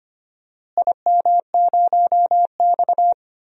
Morse code, heard as a keyed tone, about 25 wpm, 700 Hz. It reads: IM0X
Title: DXCC-100 Call Signs - 4 Characters - Repeated 3x - 25wpm